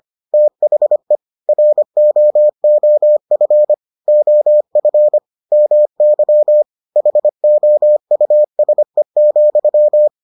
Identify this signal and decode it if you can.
Morse code, heard as a keyed tone, about 25 wpm, 600 Hz. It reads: THE ROOF OF MY HOUSE,